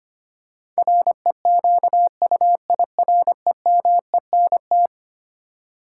Morse code, heard as a keyed tone, about 25 wpm, 700 Hz. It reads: REQUIREMENT
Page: morse-code-ninja-practice-25wpm